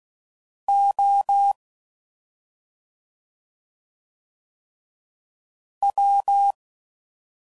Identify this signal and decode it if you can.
Morse code, heard as a keyed tone, about 16 wpm, 800 Hz. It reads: O W